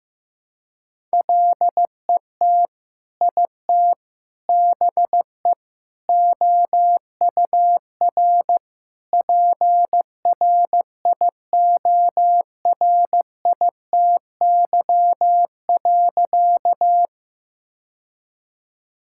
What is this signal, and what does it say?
Morse code, heard as a keyed tone, about 15 wpm, 700 Hz. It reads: LET IT BE OUR PRIORITY.
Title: Sentences from Top 1600 Words 15wpm